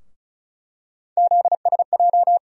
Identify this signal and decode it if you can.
Morse code, heard as a keyed tone, about 36 wpm, 700 Hz. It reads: ZSJ
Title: USA Call Sign Postfixes - 3 Letters 35wpm